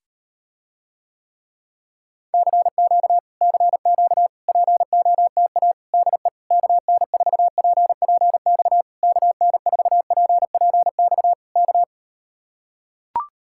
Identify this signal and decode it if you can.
Morse code, heard as a keyed tone, about 38 wpm, 700 Hz. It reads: CQ CQ POTA DE KN4PPX KN4PPX K